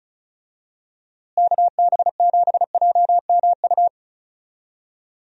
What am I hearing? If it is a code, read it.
Morse code, heard as a keyed tone, about 36 wpm, 700 Hz. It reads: KB7JMU
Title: USA Call Signs - 6 Characters - Repeated 3x - 35wpm